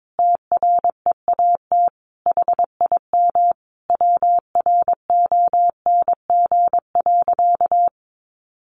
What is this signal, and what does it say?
Morse code, heard as a keyed tone, about 22 wpm, 700 Hz. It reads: TREAT HIM WRONG.
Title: Sentences from Top 1300 Words 22wpm